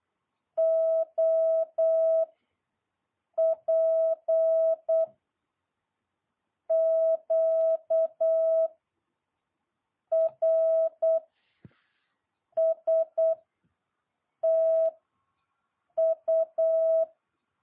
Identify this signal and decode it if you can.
Morse code, heard as a keyed tone, about 8 wpm, 650 Hz. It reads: O P Q R S T U
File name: jh8cbh